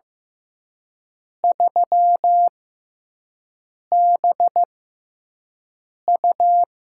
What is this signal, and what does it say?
Morse code, heard as a keyed tone, about 15 wpm, spaced as for 5 wpm, 700 Hz. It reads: 3BU
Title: USA Call Signs - 5 Characters - 6x Character Spacing - 15wpm